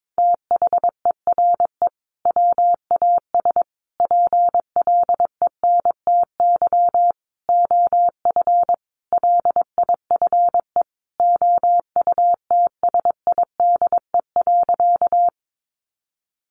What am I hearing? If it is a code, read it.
Morse code, heard as a keyed tone, about 22 wpm, 700 Hz. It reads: THERE WAS PLENTY OF LIFE OUTSIDE.
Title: Sentences from Top 1300 Words 22wpm